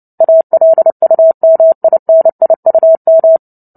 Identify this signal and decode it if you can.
Morse code, heard as a keyed tone, about 30 wpm, 650 Hz. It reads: ALUMINIUM
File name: Book Two / 03-Chapter 3 - 29 WPM